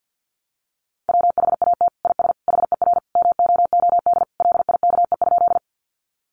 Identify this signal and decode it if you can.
Morse code, heard as a keyed tone, about 48 wpm, 700 Hz. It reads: WHAT IS HER MOOD LIKE?
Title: Sentences from Top 1100 Words 50wpm